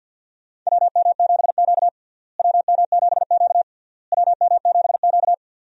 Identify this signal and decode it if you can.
Morse code, heard as a keyed tone, about 47 wpm, 700 Hz. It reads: WM7Q WM7Q WM7Q
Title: USA Call Signs - 4 Characters - Repeated 3x - 50wpm